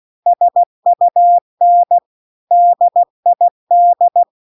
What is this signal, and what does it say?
Morse code, heard as a keyed tone, about 16 wpm, 700 Hz. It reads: SUN DID
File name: CW-The-Big-Twoheaded-River-2-16WPM